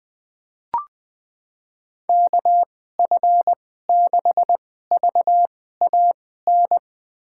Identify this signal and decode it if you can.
Morse code, heard as a keyed tone, about 20 wpm, spaced as for 14 wpm, 700 Hz. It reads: KF6VAN
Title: USA Call Signs - 6 Characters - 2x Character Spacing - 20wpm